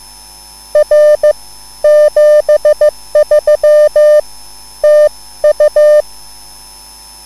Morse code, heard as a keyed tone, about 15 wpm, 600 Hz. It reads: R 73 TU
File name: QSL / CW lesson